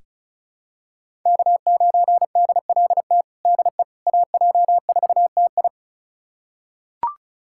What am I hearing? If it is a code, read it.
Morse code, heard as a keyed tone, about 36 wpm, 700 Hz. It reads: K9DLT DE AJ4TI
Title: USA Callsign DE Callsign - 5 Characters - 35wpm